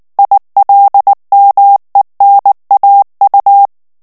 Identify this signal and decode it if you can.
Morse code, heard as a keyed tone, about 19 wpm, 800 Hz. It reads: ILMENAU
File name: morse mp3 / ALLEMAND texte clair 2 vitesse 18 mots minute